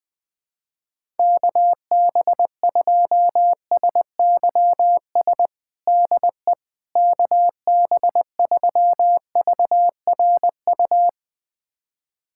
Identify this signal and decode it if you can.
Morse code, heard as a keyed tone, about 20 wpm, 700 Hz. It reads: KB2SYS DE KB3VRU
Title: USA Callsign DE Callsign - 6 Characters - 20wpm